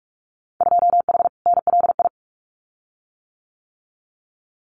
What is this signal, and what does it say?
Morse code, heard as a keyed tone, about 46 wpm, 700 Hz. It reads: 2H NLI
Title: Winter Field Day Exchanges Encore 45wpm